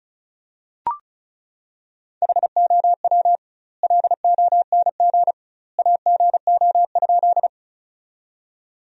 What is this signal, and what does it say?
Morse code, heard as a keyed tone, about 34 wpm, 700 Hz. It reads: HOW LONG AGO?
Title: 2x Word-Spacing - Sentences from Top 500 Words - 35wpm